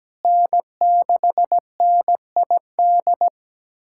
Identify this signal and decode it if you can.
Morse code, heard as a keyed tone, about 17 wpm, 700 Hz. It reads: N6NID
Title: USA Callsign DE Callsign - 5 Characters - 17wpm